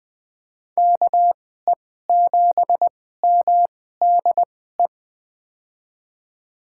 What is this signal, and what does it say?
Morse code, heard as a keyed tone, about 21 wpm, spaced as for 15 wpm, 700 Hz. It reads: KE7MDE
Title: USA Call Signs - 2x Character Spacing - 20wpm